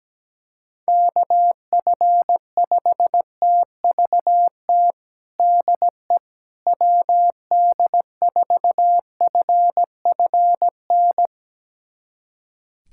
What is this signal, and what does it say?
Morse code, heard as a keyed tone, about 17 wpm, 700 Hz. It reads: KF5TVT DE WD4FFN
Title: USA Callsign DE Callsign - 6 Characters - 17wpm